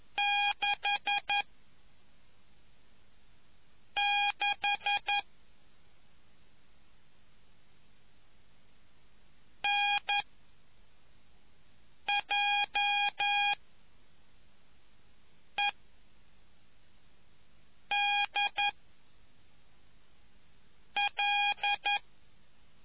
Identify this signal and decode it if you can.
Morse code, heard as a keyed tone, about 11 wpm, 2450 Hz. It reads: 6 6 N J E D L